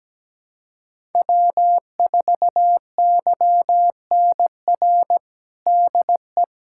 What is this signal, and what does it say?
Morse code, heard as a keyed tone, about 17 wpm, 700 Hz. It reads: W4YNR DE
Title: USA Callsign DE Callsign 17wpm